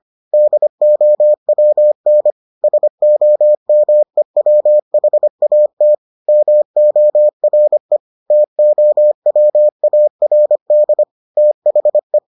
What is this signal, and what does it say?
Morse code, heard as a keyed tone, about 25 wpm, 600 Hz. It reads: DOWN SOMEWHAT MORE TOWARD THE